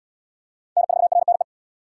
Morse code, heard as a keyed tone, about 66 wpm, 700 Hz. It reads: N4RNE